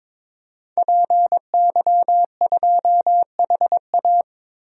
Morse code, heard as a keyed tone, about 22 wpm, 700 Hz. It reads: PY2HA
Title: DXCC-100 Call Signs - 5 Characters - 22wpm